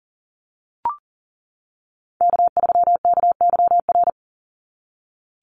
Morse code, heard as a keyed tone, about 40 wpm, 700 Hz. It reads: K3KYR
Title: NA SOTA Chaser Callsigns 40wpm